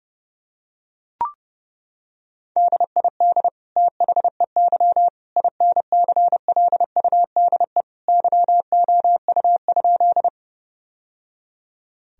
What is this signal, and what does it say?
Morse code, heard as a keyed tone, about 30 wpm, 700 Hz. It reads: DID THEY INCLUDE YOU?
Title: Sentences from Top 1000 Words - Review Part 3 - 30wpm